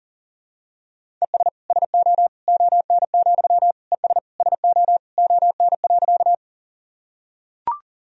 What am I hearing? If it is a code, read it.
Morse code, heard as a keyed tone, about 40 wpm, 700 Hz. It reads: ES SO ON, ES SO ON.
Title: Sentences from Top 100 Words and Abbreviations 40wpm